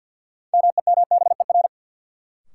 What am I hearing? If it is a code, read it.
Morse code, heard as a keyed tone, about 49 wpm, 700 Hz. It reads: MEMBER
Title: Sentences from Top 1100 Words - New Word Drill Alt - 50wpm